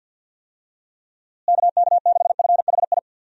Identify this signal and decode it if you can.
Morse code, heard as a keyed tone, about 48 wpm, 700 Hz. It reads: KK6FHI
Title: USA Call Signs - 6 Characters - Repeated 2x - 50wpm